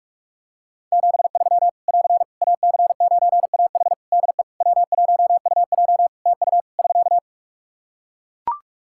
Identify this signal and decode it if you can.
Morse code, heard as a keyed tone, about 41 wpm, 700 Hz. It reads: 73 <AR> AC9AH DE W1UJ TU <SK>
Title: K1USN SST QSO Practice 45wpm